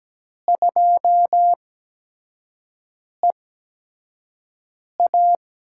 Morse code, heard as a keyed tone, about 17 wpm, spaced as for 5 wpm, 700 Hz. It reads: 2EA
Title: USA Call Signs - 4 Characters - 8x Character Spacing - 17wpm